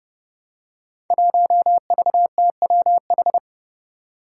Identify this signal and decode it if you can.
Morse code, heard as a keyed tone, about 30 wpm, 700 Hz. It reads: 1VTWH
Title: Five Challenging Letter-Number Combinations 30wpm